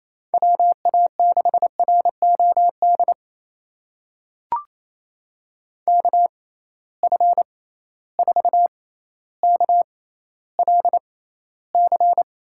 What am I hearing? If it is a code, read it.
Morse code, heard as a keyed tone, about 28 wpm, spaced as for 14 wpm, 700 Hz. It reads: WA6ROD KF4KLC
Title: USA Call Signs - 6 Characters - 6x Character Spacing - 28wpm